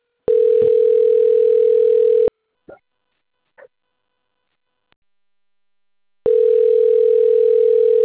Ringback tone (North America), 440 + 480 Hz.